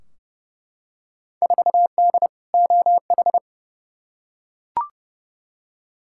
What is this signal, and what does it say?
Morse code, heard as a keyed tone, about 32 wpm, 700 Hz. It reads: 4D OH
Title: Field Day Exchanges Encore - 30wpm